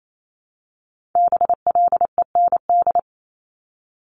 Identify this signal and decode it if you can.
Morse code, heard as a keyed tone, about 28 wpm, 700 Hz. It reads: BLEND
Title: Multi-letter Phonemes 28wpm